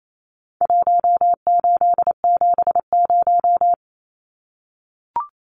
Morse code, heard as a keyed tone, about 28 wpm, 700 Hz. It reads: 1870
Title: Numbers - Contest Style - 4-Digit 28wpm